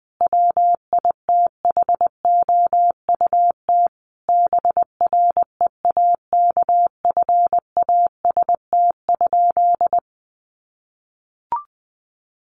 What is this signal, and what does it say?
Morse code, heard as a keyed tone, about 20 wpm, 700 Hz. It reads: WITHOUT BREAKFAST?
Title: Sentences from Top 1500 Words 20wpm